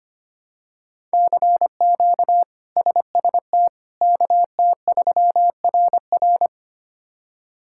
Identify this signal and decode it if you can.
Morse code, heard as a keyed tone, about 25 wpm, 700 Hz. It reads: CQ SST KT3RR
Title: K1USN SST QSO Practice - 25wpm